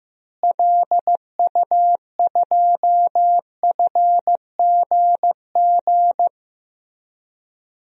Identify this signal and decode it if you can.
Morse code, heard as a keyed tone, about 15 wpm, 700 Hz. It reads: LU2FGG
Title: DXCC-100 Call Signs - 6 Characters - Repeated 2x - 15wpm